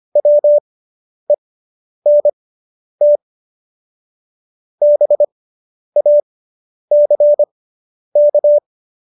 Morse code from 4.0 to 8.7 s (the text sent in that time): BACK